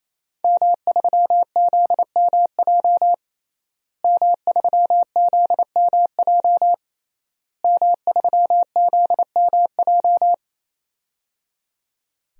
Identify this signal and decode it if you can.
Morse code, heard as a keyed tone, about 28 wpm, 700 Hz. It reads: M3ZMJ M3ZMJ M3ZMJ
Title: DXCC-100 Call Signs - 5 Characters - Repeated 3x - 28wpm